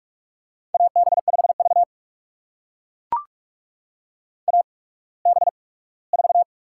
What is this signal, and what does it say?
Morse code, heard as a keyed tone, about 44 wpm, spaced as for 21 wpm, 700 Hz. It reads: AB5V AB4
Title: USA Call Signs - 4 Characters - 8x Character Spacing - 45wpm